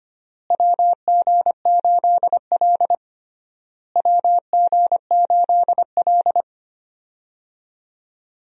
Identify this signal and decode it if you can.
Morse code, heard as a keyed tone, about 25 wpm, 700 Hz. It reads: WG8L WG8L
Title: USA Call Signs - 4 Characters - Repeated 3x - 25wpm